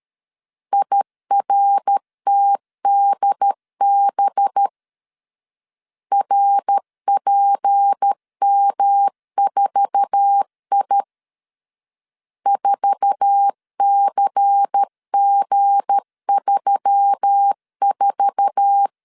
Morse code, heard as a keyed tone, about 13 wpm, 800 Hz. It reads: IRTDB RPM4I 4CG34
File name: LEZIONI / 8 YJ45 / 8 YJ4 12wpm 2151